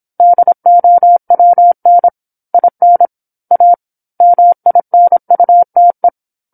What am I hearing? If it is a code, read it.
Morse code, heard as a keyed tone, about 27 wpm, 700 Hz. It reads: DOWN IN A MINUTE
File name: 08-Chapter-08 26 WPM